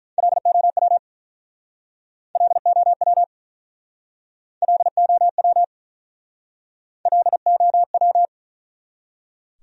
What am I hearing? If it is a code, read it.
Morse code, heard as a keyed tone, about 50 wpm, 700 Hz. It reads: LOW LOW LOW LOW